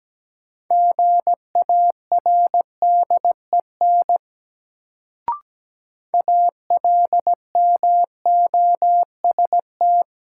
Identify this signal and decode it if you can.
Morse code, heard as a keyed tone, about 17 wpm, 700 Hz. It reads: GARDEN ALMOST
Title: Sentences from Top 700 Words - New Word Drill - 17wpm